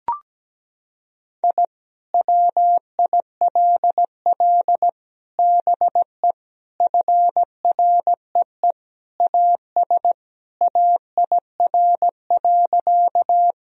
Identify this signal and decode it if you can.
Morse code, heard as a keyed tone, about 17 wpm, 700 Hz. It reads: I WILL BE FREE AS AIR.
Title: Sentences from Top 1000 Words - Review Part 1 - 17wpm